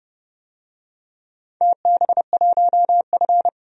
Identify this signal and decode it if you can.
Morse code, heard as a keyed tone, about 30 wpm, 700 Hz. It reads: TB1F
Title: DXCC-100 Call Signs - 4 Characters - 30wpm